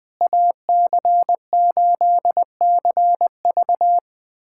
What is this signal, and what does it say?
Morse code, heard as a keyed tone, about 20 wpm, 700 Hz. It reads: AC8CV
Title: USA Callsign DE Callsign - 5 Characters - 20wpm